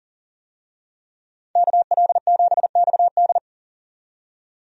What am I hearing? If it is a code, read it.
Morse code, heard as a keyed tone, about 40 wpm, 700 Hz. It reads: KL7XD
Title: DXCC-100 Call Signs - 5 Characters - 40wpm